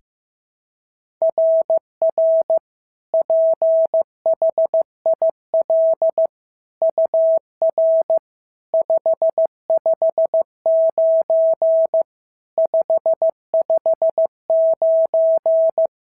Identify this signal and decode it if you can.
Morse code, heard as a keyed tone, about 14 wpm, 650 Hz. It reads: RR PHIL UR 559 559
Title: SOTA QSO Practice 15wpm